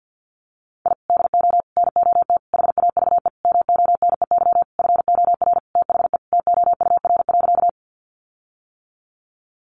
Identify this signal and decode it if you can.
Morse code, heard as a keyed tone, about 50 wpm, 700 Hz. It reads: I DO NOT HAVE MONEY FOR THE TOUR.
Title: Sentences from Top 1400 Words 50wpm